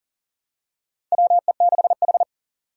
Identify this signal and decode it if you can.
Morse code, heard as a keyed tone, about 40 wpm, 700 Hz. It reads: WE6H
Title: USA Call Signs - 4 Characters - 4x Character Spacing - 40wpm